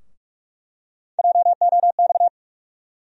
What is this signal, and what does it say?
Morse code, heard as a keyed tone, about 45 wpm, 700 Hz. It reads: JOX
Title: USA Call Sign Postfixes - 3 Letters 45wpm